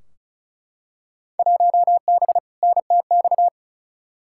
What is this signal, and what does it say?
Morse code, heard as a keyed tone, about 36 wpm, 700 Hz. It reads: 1B NTX
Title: Field Day Exchanges - 35wpm